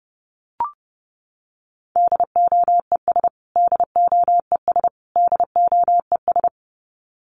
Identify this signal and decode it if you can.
Morse code, heard as a keyed tone, about 30 wpm, 700 Hz. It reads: DOES DOES DOES
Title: Top 500 Words Repeated 30wpm